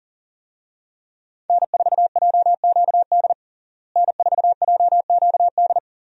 Morse code, heard as a keyed tone, about 40 wpm, 700 Hz. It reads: N4JQD N4JQD
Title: USA Call Signs - 5 Characters - Repeated 3x - 40wpm